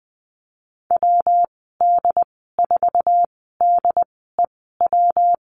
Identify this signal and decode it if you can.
Morse code, heard as a keyed tone, about 20 wpm, spaced as for 14 wpm, 700 Hz. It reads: WD4DEW
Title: 219 Lesson 35g - Number 0 - Call Signs 20wpm